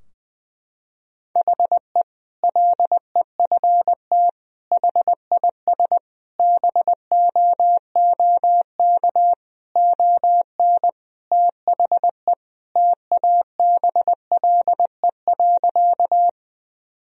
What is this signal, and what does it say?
Morse code, heard as a keyed tone, about 20 wpm, 700 Hz. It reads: HE LEFT HIS BOOK ON THE TABLE.
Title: Sentences from Top 1000 Words - Review Part 2 - 20wpm